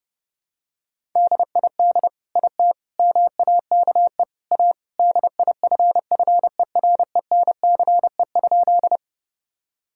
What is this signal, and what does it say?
Morse code, heard as a keyed tone, about 30 wpm, 700 Hz. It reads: DID IT MAKE A DIFFERENCE?
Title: Sentences from Top 1000 Words - Review Part 5 - 30wpm